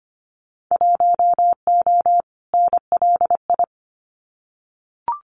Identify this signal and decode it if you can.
Morse code, heard as a keyed tone, about 25 wpm, 700 Hz. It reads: 1O NLI
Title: Winter Field Day Exchanges 25wpm